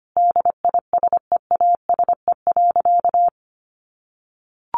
Morse code, heard as a keyed tone, about 25 wpm, 700 Hz. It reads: DISEASE.
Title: Sentences from Top 1500 Words 25wpm